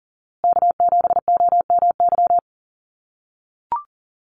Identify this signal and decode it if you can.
Morse code, heard as a keyed tone, about 40 wpm, 700 Hz. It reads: K7OMY
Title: USA Call Signs Repeated 3x 40wpm